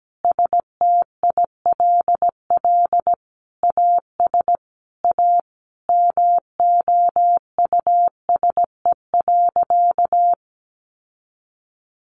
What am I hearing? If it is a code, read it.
Morse code, heard as a keyed tone, about 17 wpm, 700 Hz. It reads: STILL AS A MOUSE.
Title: Sentences from Top 700 Words 17wpm